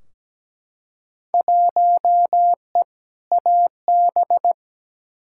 Morse code, heard as a keyed tone, about 17 wpm, 700 Hz. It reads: 1E AB
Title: Field Day Exchanges - 17wpm